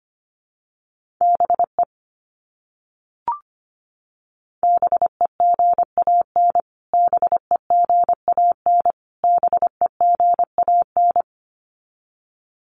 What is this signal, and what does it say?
Morse code, heard as a keyed tone, about 25 wpm, 700 Hz. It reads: BE BEGAN BEGAN BEGAN
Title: Top 500 Words Repeated 25wpm